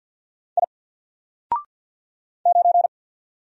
Morse code, heard as a keyed tone, about 48 wpm, 700 Hz. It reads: I 9
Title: Single Letter-Number Alt 50wpm